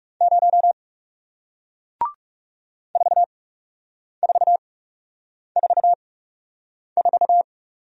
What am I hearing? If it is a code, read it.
Morse code, heard as a keyed tone, about 40 wpm, spaced as for 8 wpm, 700 Hz. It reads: 0 4444